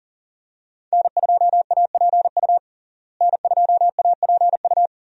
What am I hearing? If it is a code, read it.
Morse code, heard as a keyed tone, about 40 wpm, 700 Hz. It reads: N2APU N2APU
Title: USA Call Signs - 5 Characters - Repeated 2x - 40wpm